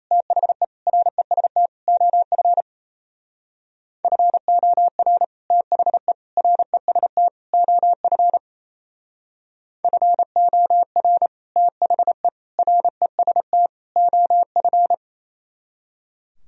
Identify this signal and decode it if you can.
Morse code, heard as a keyed tone, about 35 wpm, 700 Hz. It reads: THE REST OF FOR THE REST OF FOR THE REST OF
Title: Speed-Racing - Sets of 4 Words 28wpm